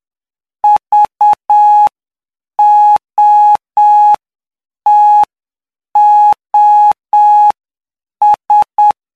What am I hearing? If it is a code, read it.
Morse code, heard as a keyed tone, about 7 wpm, spaced as for 6 wpm, 840 Hz. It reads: VOTOS
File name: cw / py2lcd / 15-PY2LCD-Li��o 24